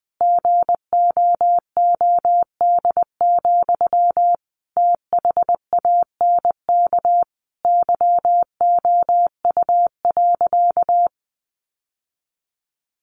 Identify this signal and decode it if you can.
Morse code, heard as a keyed tone, about 20 wpm, 700 Hz. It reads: GOOD, THANK YOU.
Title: Sentences from Top 1100 Words 20wpm